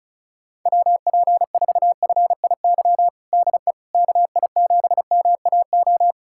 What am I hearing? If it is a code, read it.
Morse code, heard as a keyed tone, about 36 wpm, 700 Hz. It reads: WP4FIY DE KI7MAO
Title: USA Callsign DE Callsign 35wpm